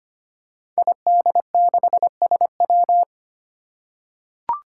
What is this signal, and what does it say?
Morse code, heard as a keyed tone, about 25 wpm, 700 Hz. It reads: ID6SW
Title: Five Challenging Letter-Number Combinations 25wpm